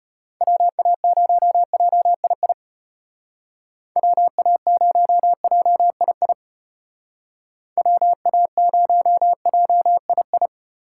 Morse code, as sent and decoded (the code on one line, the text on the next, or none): .-- .- ----- .--- .. .. / .-- .- ----- .--- .. .. / .-- .- ----- .--- .. ..
WA0JII WA0JII WA0JII